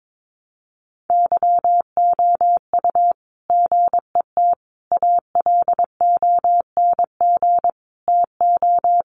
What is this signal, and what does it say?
Morse code, heard as a keyed tone, about 22 wpm, 700 Hz. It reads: YOU GET ALONG TO
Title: Sentences from Top 700 Words 22wpm